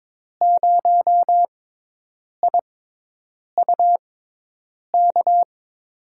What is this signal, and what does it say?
Morse code, heard as a keyed tone, about 22 wpm, spaced as for 8 wpm, 700 Hz. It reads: 0IUK
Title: USA Call Signs - 5 Characters - 6x Character Spacing - 22wpm